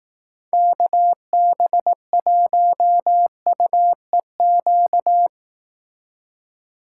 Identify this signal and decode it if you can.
Morse code, heard as a keyed tone, about 18 wpm, 700 Hz. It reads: KB1UEQ
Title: Speed Racing - USA 2x3 Call Signs 15wpm